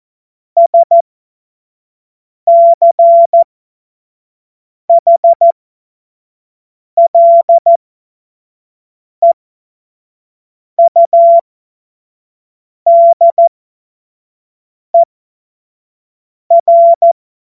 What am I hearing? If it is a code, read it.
Morse code, heard as a keyed tone, about 14 wpm, spaced as for 5 wpm, 680 Hz. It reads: SCHLEUDER
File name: Morsekurs / Witze / Witz 15. August 2023 25 F70